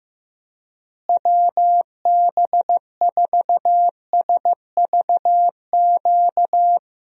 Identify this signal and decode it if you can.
Morse code, heard as a keyed tone, about 15 wpm, 700 Hz. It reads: WB4SVQ
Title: USA Call Signs - 6 Characters - Repeated 3x - 15wpm